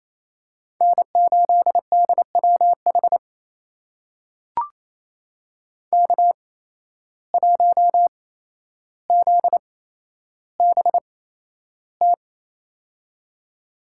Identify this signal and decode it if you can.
Morse code, heard as a keyed tone, about 28 wpm, spaced as for 12 wpm, 700 Hz. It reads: N8DWH K1ZBT
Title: USA Call Signs - 5 Characters - 8x Character Spacing - 28wpm